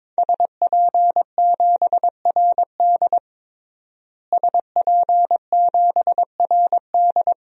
Morse code, heard as a keyed tone, about 22 wpm, 700 Hz. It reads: SP7RD SP7RD
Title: DXCC-100 Call Signs - 5 Characters - Repeated 2x - 22wpm